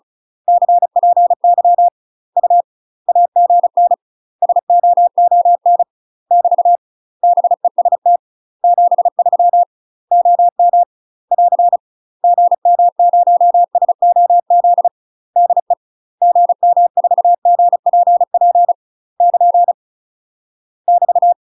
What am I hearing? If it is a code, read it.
Morse code, heard as a keyed tone, about 34 wpm, 700 Hz. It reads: CPY U AGN SOON <BT> BEST 73 OM <AR> GM0SOZ DE GM4GPP <KN> <BT>
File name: UK MASTER QSOs TRAINING / 35WPM / UK MASTER QSOs 0043